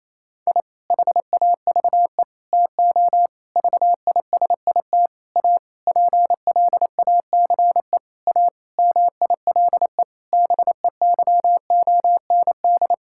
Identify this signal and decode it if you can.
Morse code, heard as a keyed tone, about 28 wpm, 700 Hz. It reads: I HAVE TO VISIT A PLACE A MILE BEYOND
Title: Sentences from Top 1000 Words - Review Part 6 - 28wpm